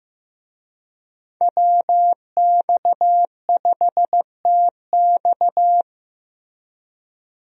Morse code, heard as a keyed tone, about 15 wpm, 700 Hz. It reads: WX5TX
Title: USA Call Signs - 5 Characters - Repeated 3x - 15wpm